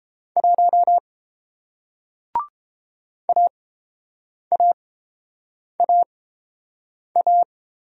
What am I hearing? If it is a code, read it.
Morse code, heard as a keyed tone, about 30 wpm, spaced as for 7 wpm, 700 Hz. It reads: 1 AAAA